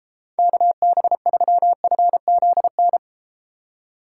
Morse code, heard as a keyed tone, about 31 wpm, 700 Hz. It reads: KB3FZN